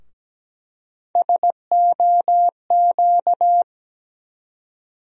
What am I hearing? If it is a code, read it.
Morse code, heard as a keyed tone, about 17 wpm, 700 Hz. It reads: SOQ
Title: USA Call Sign Postfixes - 3 Letters 17wpm